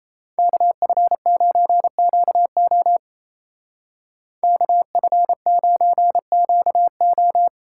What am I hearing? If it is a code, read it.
Morse code, heard as a keyed tone, about 37 wpm, 700 Hz. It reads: KF9QO KF9QO